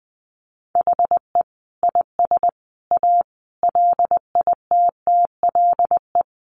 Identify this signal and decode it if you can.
Morse code, heard as a keyed tone, about 19 wpm, 700 Hz. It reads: HE IS A LITTLE
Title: Sentences from Top 600 Words 20wpm